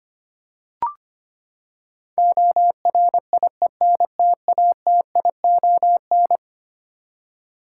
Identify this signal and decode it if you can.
Morse code, heard as a keyed tone, about 25 wpm, 700 Hz. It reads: ORIENTATION
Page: morse-code-ninja-practice-25wpm